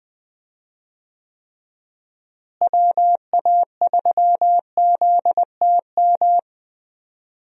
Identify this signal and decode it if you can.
Morse code, heard as a keyed tone, about 20 wpm, 700 Hz. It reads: WA3ZTM
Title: USA Call Signs - 6 Characters - Repeated 2x - 20wpm